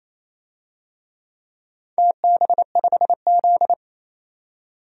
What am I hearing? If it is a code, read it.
Morse code, heard as a keyed tone, about 28 wpm, 700 Hz. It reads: TB5Z
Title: DXCC-100 Call Signs - 4 Characters - 28wpm